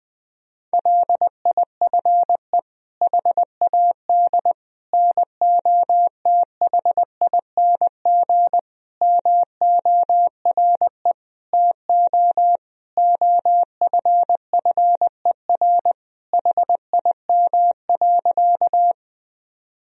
Morse code, heard as a keyed tone, about 21 wpm, 700 Hz. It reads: LIFE HAD NOTHING MORE TO OFFER HIM.
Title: Sentences from Top 1000 Words - Review Part 4 - 20wpm